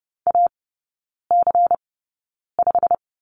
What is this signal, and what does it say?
Morse code, heard as a keyed tone, about 30 wpm, spaced as for 9 wpm, 700 Hz. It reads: AC5
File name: Lesson 16e - Letter c - Three Characters 30wpm